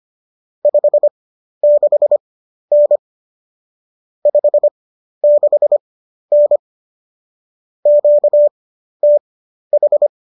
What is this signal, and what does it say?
Morse code, heard as a keyed tone, about 25 wpm, spaced as for 12 wpm, 590 Hz. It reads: 56N 56N QTH
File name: qso008 12